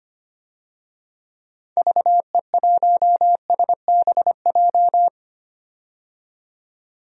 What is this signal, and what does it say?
Morse code, heard as a keyed tone, about 25 wpm, 700 Hz. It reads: VE1SBJ
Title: DXCC-100 Call Signs - 6 Characters - 25wpm